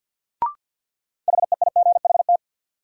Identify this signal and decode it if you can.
Morse code, heard as a keyed tone, about 53 wpm, 700 Hz. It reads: HEIGHT
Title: Sentences from Top 1600 Words - New Word Drill Alt - 50wpm